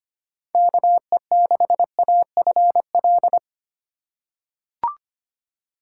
Morse code, heard as a keyed tone, about 25 wpm, 700 Hz. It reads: KE6AFL
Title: USA Call Signs - 6 Characters - 3x Character Spacing - 25wpm